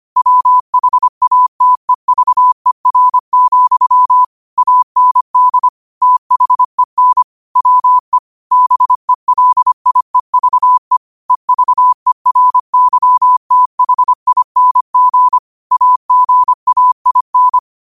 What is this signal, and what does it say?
Morse code, heard as a keyed tone, about 25 wpm, 1000 Hz. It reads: WHATEVER, AND THEN WE BELIEVE EVERYTHING AGAIN
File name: qotdfiles / 25wpm